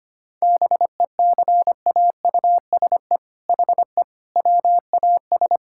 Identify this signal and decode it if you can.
Morse code, heard as a keyed tone, about 25 wpm, 700 Hz. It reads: BECAUSE HE WAS
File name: practice / 25wpm / All-Age Appropriate Jokes - 25wpm